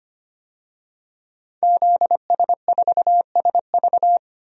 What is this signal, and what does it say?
Morse code, heard as a keyed tone, about 25 wpm, 700 Hz. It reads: ZS4SV
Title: DXCC-100 Call Signs - 5 Characters - 25wpm